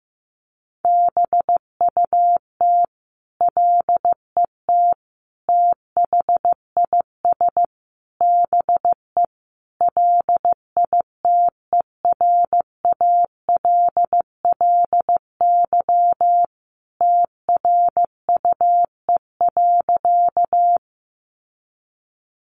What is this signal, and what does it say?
Morse code, heard as a keyed tone, about 15 wpm, 700 Hz. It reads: BUT LET THIS BE LITERALLY TRUE.
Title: Sentences from Top 1400 Words 15wpm